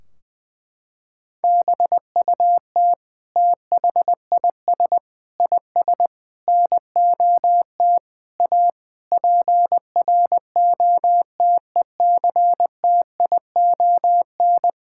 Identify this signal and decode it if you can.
Morse code, heard as a keyed tone, about 20 wpm, 700 Hz. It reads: BUT THIS IS NOT A PROTECTION
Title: Sentences from Top 1600 Words 20wpm